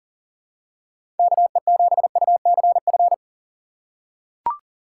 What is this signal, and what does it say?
Morse code, heard as a keyed tone, about 39 wpm, 700 Hz. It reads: KE7UCF
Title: USA Call Signs - 3x Character Spacing - 40wpm